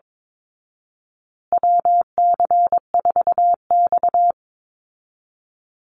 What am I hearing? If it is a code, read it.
Morse code, heard as a keyed tone, about 22 wpm, 700 Hz. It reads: WC4X
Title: Call Signs Repeated 2x - 22wpm